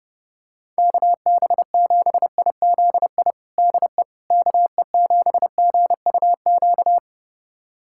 Morse code, heard as a keyed tone, about 30 wpm, 700 Hz. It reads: KB7IZI DE KE7GUQ
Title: USA Callsign DE Callsign - 6 Characters - 30wpm